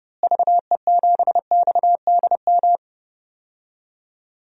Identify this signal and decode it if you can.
Morse code, heard as a keyed tone, about 30 wpm, 700 Hz. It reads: VE7XDM
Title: DXCC-100 Call Signs - 6 Characters - Repeated 3x - 30wpm